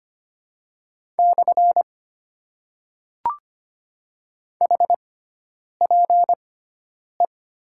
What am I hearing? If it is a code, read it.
Morse code, heard as a keyed tone, about 25 wpm, spaced as for 9 wpm, 700 Hz. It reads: / HPE
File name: Top 90 QSO Elements - 6x Character Spacing - 25wpm